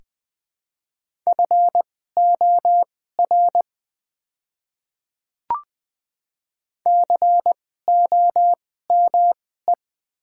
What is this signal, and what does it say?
Morse code, heard as a keyed tone, about 20 wpm, spaced as for 14 wpm, 700 Hz. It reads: FOR COME
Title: 2x Character Spacing - Top 100 Words - 20wpm